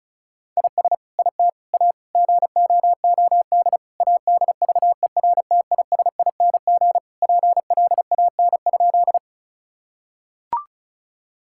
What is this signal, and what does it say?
Morse code, heard as a keyed tone, about 36 wpm, 700 Hz. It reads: IS IT A GOOD ADVERTISING PLAN?
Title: Sentences from Top 1000 Words - Review Part 6 - 35wpm